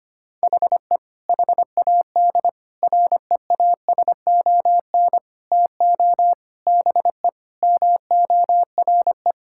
Morse code, heard as a keyed tone, about 25 wpm, 700 Hz. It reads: HE HAD REASON TO BE MORE
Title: Sentences from Top 1000 Words - Review Part 6 - 25wpm